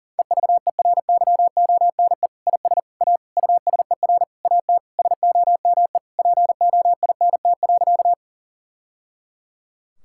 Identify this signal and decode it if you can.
Morse code, heard as a keyed tone, about 37 wpm, 700 Hz. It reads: EVERYONE IS A USER AT SOME POINT.
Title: Sentences from Top 1000 Words - Review Part 5 - 40wpm